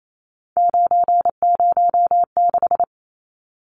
Morse code, heard as a keyed tone, about 28 wpm, 700 Hz. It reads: 906
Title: Numbers - 3-Digit 28wpm